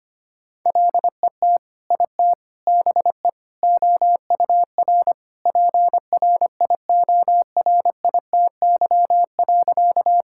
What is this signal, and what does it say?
Morse code, heard as a keyed tone, about 25 wpm, 700 Hz. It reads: LET IT BE OUR PRIORITY.
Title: Sentences from Top 1600 Words 25wpm